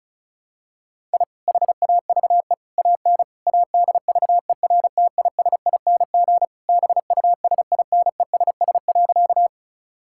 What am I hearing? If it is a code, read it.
Morse code, heard as a keyed tone, about 34 wpm, 700 Hz. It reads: I HAVE AN ADVERTISING BUSINESS.
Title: Sentences from Top 1000 Words - Review Part 6 - 35wpm